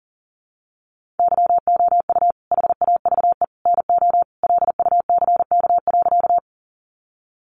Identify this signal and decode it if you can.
Morse code, heard as a keyed tone, about 37 wpm, 700 Hz. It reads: YOU HAVE NO LUCK.
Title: Sentences from Top 1500 Words 40wpm